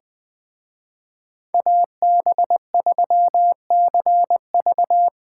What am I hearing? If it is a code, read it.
Morse code, heard as a keyed tone, about 24 wpm, 700 Hz. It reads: AB3CV
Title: Speed Racing - USA 2x2 Call Signs 20wpm